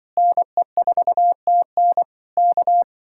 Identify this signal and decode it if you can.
Morse code, heard as a keyed tone, about 24 wpm, 700 Hz. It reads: NE4TN K
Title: Speed-Racing Common QSO Phrases Encore - 20wpm